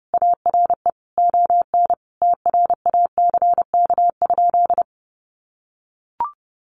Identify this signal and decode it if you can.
Morse code, heard as a keyed tone, about 30 wpm, 700 Hz. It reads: ARE ON TRACK?
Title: Sentences from Top 1200 Words 30wpm